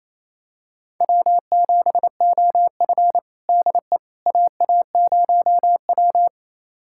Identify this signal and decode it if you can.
Morse code, heard as a keyed tone, about 28 wpm, 700 Hz. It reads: W7OF DE AA0W
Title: USA Callsign DE Callsign - 4 Characters - 28wpm